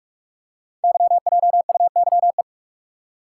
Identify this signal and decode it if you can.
Morse code, heard as a keyed tone, about 44 wpm, 700 Hz. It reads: YJUYE
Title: Code Groups - 5 Letters - 45wpm